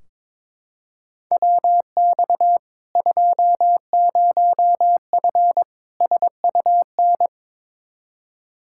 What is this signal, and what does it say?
Morse code, heard as a keyed tone, about 22 wpm, 700 Hz. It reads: WX 20F SUN
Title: Practice QSO - Exchange 2 - Info - 22wpm